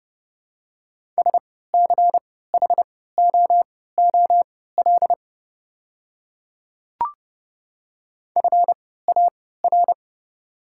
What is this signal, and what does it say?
Morse code, heard as a keyed tone, about 32 wpm, spaced as for 17 wpm, 700 Hz. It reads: SCHOOL FAR
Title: 3x Character Spacing - Top 300 Words - 30wpm